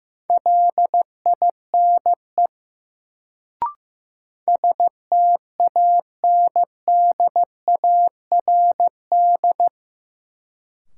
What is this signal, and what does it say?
Morse code, heard as a keyed tone, about 15 wpm, 700 Hz. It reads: LINE STANDARD
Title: Sentences from Top 1200 Words - New Word Drill Alt - 15wpm